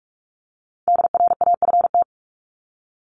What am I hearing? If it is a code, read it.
Morse code, heard as a keyed tone, about 44 wpm, 700 Hz. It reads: DRAFT
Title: Top 1751-2000 Words 45wpm